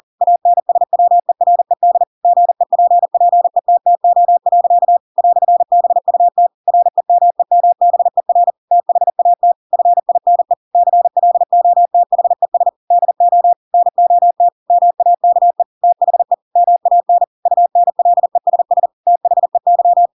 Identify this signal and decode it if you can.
Morse code, heard as a keyed tone, about 40 wpm, 700 Hz. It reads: ANSWERED GEPPETTO. "BUT REMEMBER THAT FINE CLOTHES DO NOT MAKE THE MAN UNLESS THEY
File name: Pinocchio / Pinocchio ch8 40 40 ik1hge